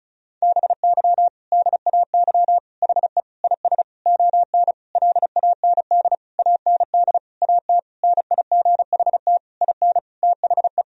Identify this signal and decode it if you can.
Morse code, heard as a keyed tone, about 36 wpm, 700 Hz. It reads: BY DAY HE IS ON LAND AND AT NIGHT IN THE
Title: Sentences from Top 1000 Words - Review Part 1 - 35wpm